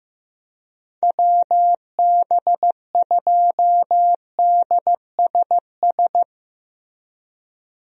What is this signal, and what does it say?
Morse code, heard as a keyed tone, about 15 wpm, 700 Hz. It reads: WB2DSS
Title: USA Call Signs - 6 Characters - Repeated 2x - 15wpm